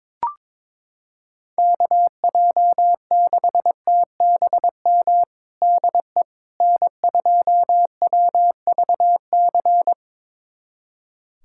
Morse code, heard as a keyed tone, about 22 wpm, 700 Hz. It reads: KJ6TBM DE N2WVC
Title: USA Callsign DE Callsign 22wpm